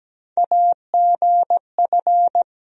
Morse code, heard as a keyed tone, about 17 wpm, 700 Hz. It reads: AGF
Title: USA Call Sign Postfixes - 3 Letters 17wpm